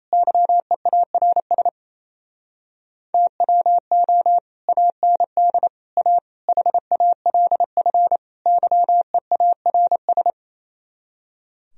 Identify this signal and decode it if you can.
Morse code, heard as a keyed tone, about 27 wpm, 700 Hz. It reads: YEARS TWO AND A HALF YEARS
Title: Speed-Racing - Sets of 5 Words 28wpm